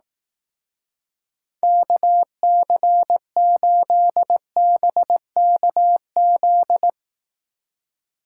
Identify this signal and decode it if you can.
Morse code, heard as a keyed tone, about 18 wpm, 700 Hz. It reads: KC8BKZ
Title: Speed Racing - USA Call Signs 15wpm